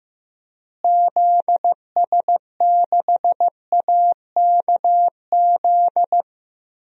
Morse code, heard as a keyed tone, about 15 wpm, 700 Hz. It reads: ZS6AKZ
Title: DXCC-100 Call Signs - 6 Characters - Repeated 3x - 15wpm